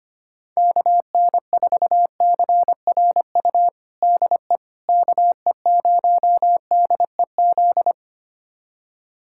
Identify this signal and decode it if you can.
Morse code, heard as a keyed tone, about 25 wpm, 700 Hz. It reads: KN4CRU DE KE0DEZ
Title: USA Callsign DE Callsign - 6 Characters - 25wpm